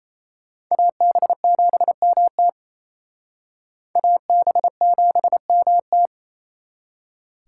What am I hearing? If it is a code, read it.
Morse code, heard as a keyed tone, about 34 wpm, 700 Hz. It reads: AB7MT AB7MT
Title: Speed Racing - USA Call Signs 28wpm